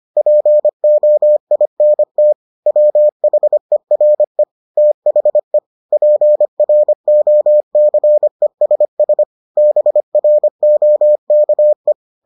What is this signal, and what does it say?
Morse code, heard as a keyed tone, about 25 wpm, 600 Hz. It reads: POINT WHERE THE PROCESS BROKE